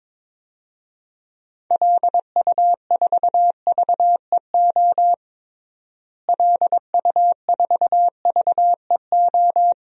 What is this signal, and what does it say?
Morse code, heard as a keyed tone, about 21 wpm, 700 Hz. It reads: LU4VEO LU4VEO
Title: DXCC-100 Call Signs - 6 Characters - Repeated 2x - 22wpm